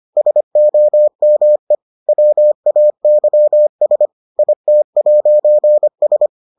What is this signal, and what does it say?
Morse code, heard as a keyed tone, about 25 wpm, 600 Hz. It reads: SOME WAYS IT'S